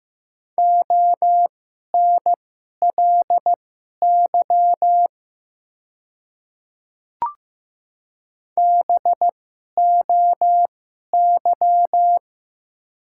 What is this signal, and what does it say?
Morse code, heard as a keyed tone, about 15 wpm, spaced as for 11 wpm, 700 Hz. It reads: ONLY BOY
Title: 2x Character Spacing - Top 200 Words - 15wpm